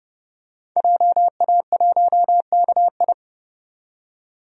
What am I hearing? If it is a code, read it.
Morse code, heard as a keyed tone, about 30 wpm, 700 Hz. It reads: JA1KI
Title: DXCC-100 Call Signs - 5 Characters - 30wpm